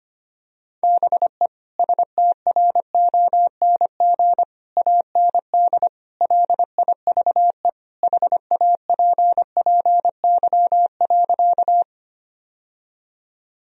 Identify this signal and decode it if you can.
Morse code, heard as a keyed tone, about 25 wpm, 700 Hz. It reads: BE STRONG AND LIVE HAPPY.
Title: Sentences from Top 1000 Words - Review Part 3 - 25wpm